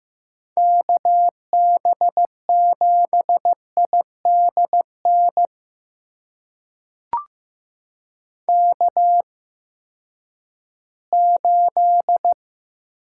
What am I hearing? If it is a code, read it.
Morse code, heard as a keyed tone, about 15 wpm, spaced as for 7 wpm, 700 Hz. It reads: KB7ID NK8
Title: USA Call Signs - 8x Character Spacing - 15wpm